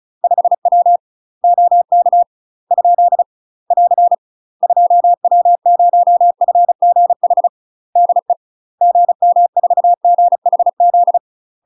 Morse code, heard as a keyed tone, about 36 wpm, 700 Hz. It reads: 5W OK ? <AR> 2W0FGH DE GM4GHZ
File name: UK MASTER QSOs TRAINING / 35WPM / UK MASTER QSOs 0002